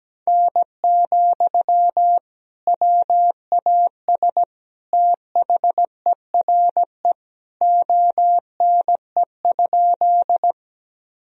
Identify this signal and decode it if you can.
Morse code, heard as a keyed tone, about 17 wpm, 700 Hz. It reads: N, WAS THERE ONE?
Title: Sentences from Top 100 Words and Abbreviations 17wpm